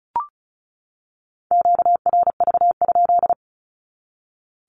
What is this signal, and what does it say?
Morse code, heard as a keyed tone, about 34 wpm, 700 Hz. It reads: QRV?
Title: Q-Codes 35wpm